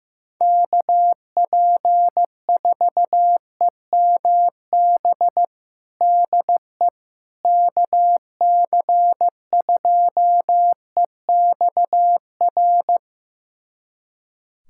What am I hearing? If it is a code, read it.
Morse code, heard as a keyed tone, about 15 wpm, 700 Hz. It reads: KP4EMB DE KC2EXR
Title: USA Callsign DE Callsign 15wpm